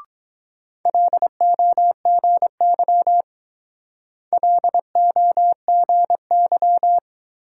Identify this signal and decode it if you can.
Morse code, heard as a keyed tone, about 21 wpm, 700 Hz. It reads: LOGY LOGY